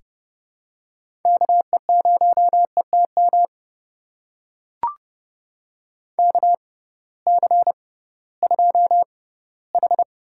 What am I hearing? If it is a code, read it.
Morse code, heard as a keyed tone, about 29 wpm, spaced as for 17 wpm, 700 Hz. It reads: KE0ETM KC2H
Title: USA Call Signs - 6 Characters - 6x Character Spacing - 30wpm